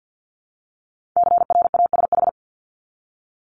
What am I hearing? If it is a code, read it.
Morse code, heard as a keyed tone, about 50 wpm, 700 Hz. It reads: CRASH
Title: Multi-letter Phonemes Repeated 3x 50wpm